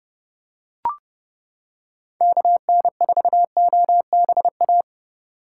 Morse code, heard as a keyed tone, about 30 wpm, 700 Hz. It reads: KN4OBA